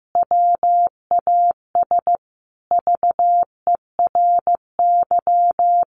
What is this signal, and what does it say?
Morse code, heard as a keyed tone, about 15 wpm, 700 Hz. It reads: WAS VERY
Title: Sentences from Top 1400 Words 15wpm